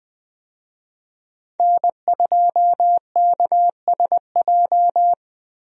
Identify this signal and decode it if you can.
Morse code, heard as a keyed tone, about 20 wpm, 700 Hz. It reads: N2KSJ
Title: USA Call Signs - 5 Characters - Repeated 3x - 20wpm